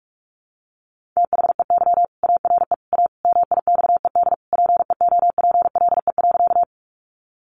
Morse code, heard as a keyed tone, about 44 wpm, 700 Hz. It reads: THEY ARE A MIXED PEOPLE.
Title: Sentences from Top 1500 Words 45wpm